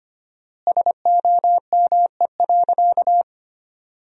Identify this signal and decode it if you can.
Morse code, heard as a keyed tone, about 25 wpm, 700 Hz. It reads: SOME.
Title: 4x Word-Spacing - Sentences from Top 100 Words - 25wpm